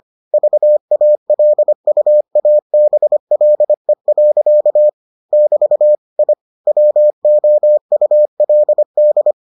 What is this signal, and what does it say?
Morse code, heard as a keyed tone, about 25 wpm, 600 Hz. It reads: VALUABLE. <BT> I WOULD